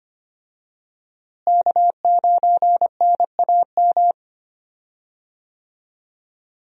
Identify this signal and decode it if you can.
Morse code, heard as a keyed tone, about 25 wpm, 700 Hz. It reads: K9NAM
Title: USA Call Signs - 5 Characters - Repeated 3x - 25wpm